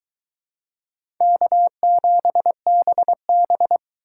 Morse code, heard as a keyed tone, about 22 wpm, 700 Hz. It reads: K7BB